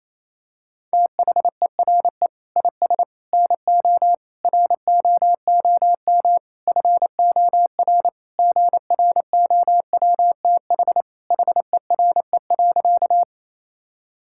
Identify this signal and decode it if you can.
Morse code, heard as a keyed tone, about 28 wpm, 700 Hz. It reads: THERE IS NO ROOM FOR GROWTH HERE.
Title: Sentences from Top 1000 Words - Review Part 5 - 28wpm